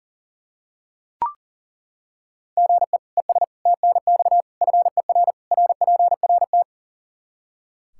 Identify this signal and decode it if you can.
Morse code, heard as a keyed tone, about 40 wpm, 700 Hz. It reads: GE ES TNX FER RPRT
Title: Practice QSO - Exchange 1 - Beginning - 40wpm